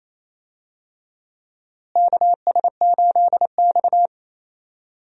Morse code, heard as a keyed tone, about 28 wpm, 700 Hz. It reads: KS8X
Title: USA Call Signs - 4 Characters - Repeated 3x - 28wpm